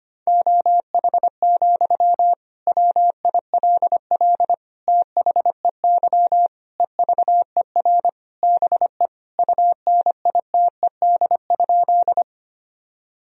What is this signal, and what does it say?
Morse code, heard as a keyed tone, about 25 wpm, 700 Hz. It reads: OH, WILL THEY EVER BE UNITED?
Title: Sentences from Top 1000 Words - Review Part 4 - 25wpm